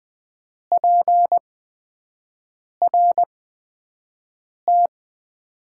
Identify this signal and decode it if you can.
Morse code, heard as a keyed tone, about 20 wpm, spaced as for 5 wpm, 700 Hz. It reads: PRT
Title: Top 90 QSO Elements - 8x Character Spacing - 20wpm